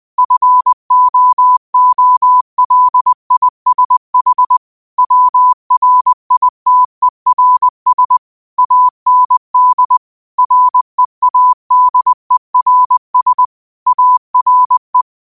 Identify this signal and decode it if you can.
Morse code, heard as a keyed tone, about 20 wpm, 1000 Hz. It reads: FOOLISH WRITERS AND READERS ARE
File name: qotdfiles / 20wpm